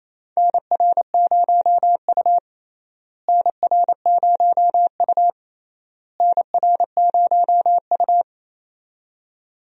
Morse code, heard as a keyed tone, about 28 wpm, 700 Hz. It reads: NR0U NR0U NR0U
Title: USA Call Signs - 4 Characters - Repeated 3x - 28wpm